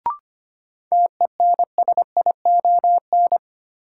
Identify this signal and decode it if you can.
Morse code, heard as a keyed tone, about 25 wpm, 700 Hz. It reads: TENSION
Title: Sentences from Top 1600 Words - New Word Drill Alt - 25wpm